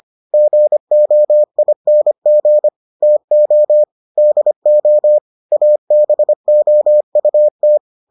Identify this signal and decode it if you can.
Morse code, heard as a keyed tone, about 25 wpm, 600 Hz. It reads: GOING TO DO ABOUT